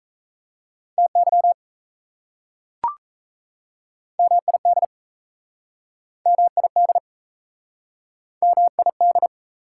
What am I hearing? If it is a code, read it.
Morse code, heard as a keyed tone, about 37 wpm, 700 Hz. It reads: TY MID MID MID